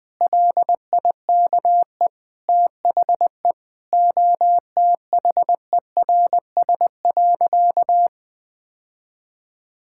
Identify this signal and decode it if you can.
Morse code, heard as a keyed tone, about 20 wpm, 700 Hz. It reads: LIKE THE OTHERS.
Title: Sentences with 8-Word Phrases - Spoken Beginning - 20wpm